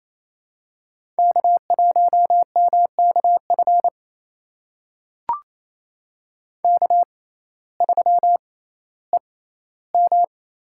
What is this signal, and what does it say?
Morse code, heard as a keyed tone, about 27 wpm, spaced as for 15 wpm, 700 Hz. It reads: K1MKF K3EM
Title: USA Call Signs - 5 Characters - 6x Character Spacing - 28wpm